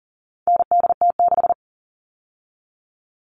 Morse code, heard as a keyed tone, about 40 wpm, 700 Hz. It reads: NDT6
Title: All Cut Numbers - 4-Digit 40wpm